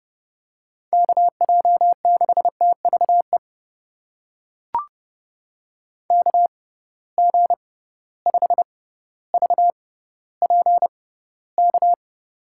Morse code, heard as a keyed tone, about 30 wpm, spaced as for 15 wpm, 700 Hz. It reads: KJ6TVE KG5VPK
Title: USA Call Signs - 6 Characters - 6x Character Spacing - 30wpm